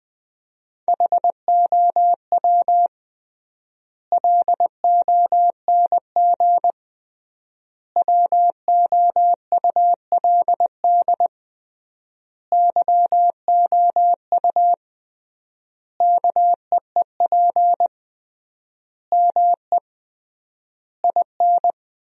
Morse code, heard as a keyed tone, about 21 wpm, 700 Hz. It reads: HOW LONG WOULD YOU KEEP ME IN
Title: 3x Word-Spacing - Sentences from Top 500 Words - 20wpm